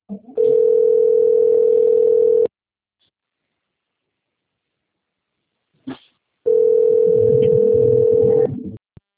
Ringback tone (North America), 440 + 480 Hz.